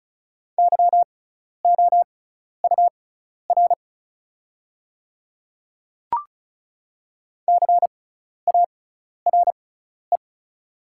Morse code, heard as a keyed tone, about 36 wpm, spaced as for 12 wpm, 690 Hz. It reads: YOUR CARE
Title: 6x Character Spacing - Top 300 Words - 35wpm